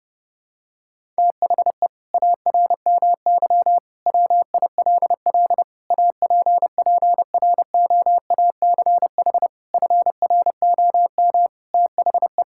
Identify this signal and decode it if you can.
Morse code, heard as a keyed tone, about 30 wpm, 700 Hz. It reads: THE ARMY WILL APPROACH FROM THE
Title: Sentences from Top 1000 Words - Review Part 6 - 30wpm